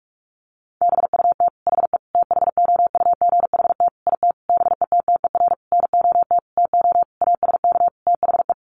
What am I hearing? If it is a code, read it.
Morse code, heard as a keyed tone, about 44 wpm, 700 Hz. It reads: BUT HE THOUGHT IT BETTER NOT TO ASK THE
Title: Sentences from Top 600 Words 45wpm